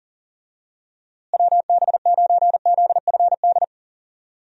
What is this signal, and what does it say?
Morse code, heard as a keyed tone, about 40 wpm, 700 Hz. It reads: WB9ZFD
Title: USA Call Signs - 6 Characters - Repeated 2x - 40wpm